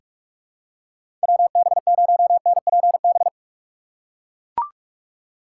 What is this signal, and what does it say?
Morse code, heard as a keyed tone, about 43 wpm, 700 Hz. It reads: WB0NPB
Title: USA Call Signs - 2x Character Spacing - 45wpm